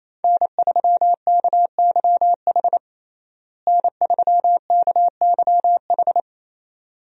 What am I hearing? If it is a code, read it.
Morse code, heard as a keyed tone, about 28 wpm, 700 Hz. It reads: N3KYH N3KYH
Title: USA Call Signs - 5 Characters - Repeated 2x - 28wpm